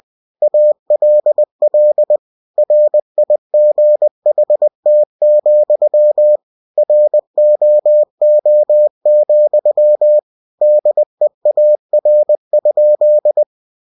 Morse code, heard as a keyed tone, about 20 wpm, 600 Hz. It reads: ALL RIGHT, ROO, DEAR?
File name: cw-books / winnie-the-pooh / winnie-20wpm-page-0069